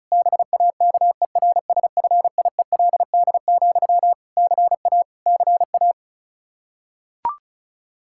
Morse code, heard as a keyed tone, about 34 wpm, 700 Hz. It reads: BAKERSFIELD, CA CA
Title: Practice QSO - Exchange 1 - Beginning - 35wpm